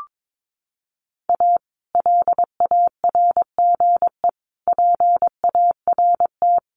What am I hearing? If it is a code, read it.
Morse code, heard as a keyed tone, about 22 wpm, 700 Hz. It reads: A LARGE PART
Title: Sets of 5 Words Encore2 22wpm